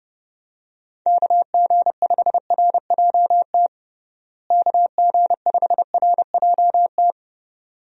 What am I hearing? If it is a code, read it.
Morse code, heard as a keyed tone, about 30 wpm, 700 Hz. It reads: KG5RJT KG5RJT
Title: USA Call Signs - 6 Characters - Repeated 2x - 30wpm